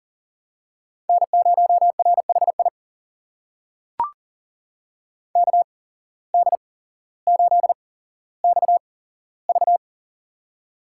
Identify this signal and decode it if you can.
Morse code, heard as a keyed tone, about 40 wpm, spaced as for 17 wpm, 700 Hz. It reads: N0RHI KD8XV
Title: USA Call Signs - 5 Characters - 8x Character Spacing - 40wpm